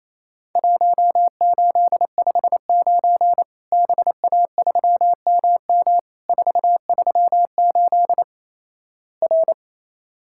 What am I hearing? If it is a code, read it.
Morse code, heard as a keyed tone, about 28 wpm, 700 Hz. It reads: 1859 BA3MM 438 R
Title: Worked All Europe DX Contest - QTC Exchange - 28wpm